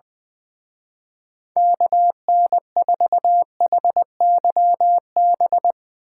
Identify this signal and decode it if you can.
Morse code, heard as a keyed tone, about 21 wpm, 700 Hz. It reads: KN4HYB
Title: Speed Racing - USA 2x3 Call Signs 20wpm